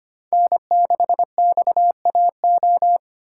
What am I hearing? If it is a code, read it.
Morse code, heard as a keyed tone, about 25 wpm, 700 Hz. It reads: N6XAO
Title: USA Call Signs - 5 Characters - 6x Character Spacing - 25wpm